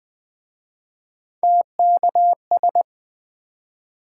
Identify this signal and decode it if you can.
Morse code, heard as a keyed tone, about 23 wpm, 700 Hz. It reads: TKS